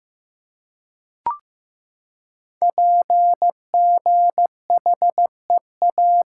Morse code, heard as a keyed tone, about 15 wpm, 700 Hz. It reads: PGHEA
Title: Code Groups - 5 Letters - 15wpm